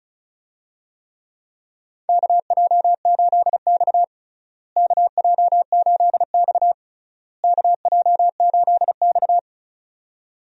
Morse code, heard as a keyed tone, about 36 wpm, 700 Hz. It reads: KJ8X KJ8X KJ8X
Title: USA Call Signs - 4 Characters - Repeated 3x - 35wpm